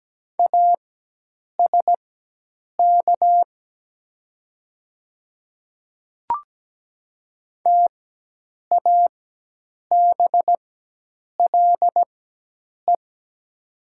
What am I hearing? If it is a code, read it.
Morse code, heard as a keyed tone, about 17 wpm, spaced as for 8 wpm, 700 Hz. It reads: ASK TABLE
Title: 4x Character Spacing - Top 500 Words - 17wpm